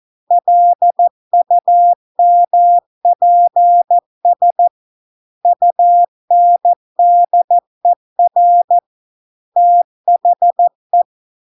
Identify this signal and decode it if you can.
Morse code, heard as a keyed tone, about 14 wpm, 700 Hz. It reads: LUMPS UNDER THE